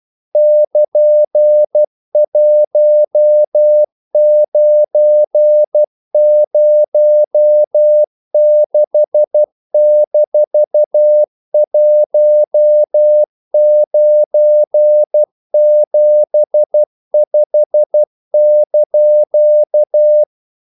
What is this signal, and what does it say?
Morse code, heard as a keyed tone, about 12 wpm, 600 Hz. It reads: <KN>1906-1975)
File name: CW / Practice / EnglishQuotations0312